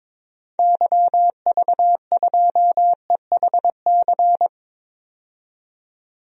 Morse code, heard as a keyed tone, about 22 wpm, 700 Hz. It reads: YV2EHC
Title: DXCC-100 Call Signs - 6 Characters - 22wpm